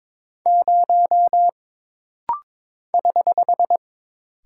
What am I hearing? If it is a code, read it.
Morse code, heard as a keyed tone, about 22 wpm, 700 Hz. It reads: 0 <HH>
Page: This rapid-fire timing is geared to push your ICR (Instant Character Recognition) skills.